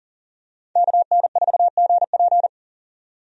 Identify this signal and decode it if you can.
Morse code, heard as a keyed tone, about 40 wpm, 700 Hz. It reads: KN4GP
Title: USA Call Signs - 5 Characters - 40wpm